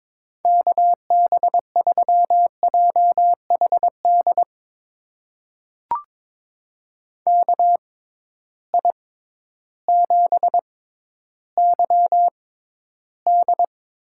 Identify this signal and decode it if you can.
Morse code, heard as a keyed tone, about 21 wpm, spaced as for 12 wpm, 700 Hz. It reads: KB3JHD KI7YD
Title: USA Call Signs - 6 Characters - 6x Character Spacing - 22wpm